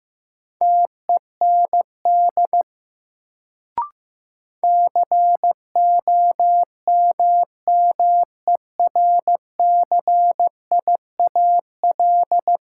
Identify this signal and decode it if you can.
Morse code, heard as a keyed tone, about 15 wpm, 700 Hz. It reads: TEND COMMERCIAL
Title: Sentences from Top 1000 Words - Review Part 5 - New Word Drill - 15wpm